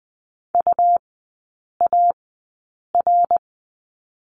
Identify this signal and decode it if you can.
Morse code, heard as a keyed tone, about 20 wpm, spaced as for 8 wpm, 700 Hz. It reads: UAR
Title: 100 Lesson 17e - Letter u - Three Characters 20wpm